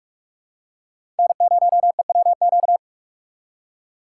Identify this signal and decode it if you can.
Morse code, heard as a keyed tone, about 46 wpm, 700 Hz. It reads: N0EWQ
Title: USA Call Signs - 5 Characters - 2x Character Spacing - 45wpm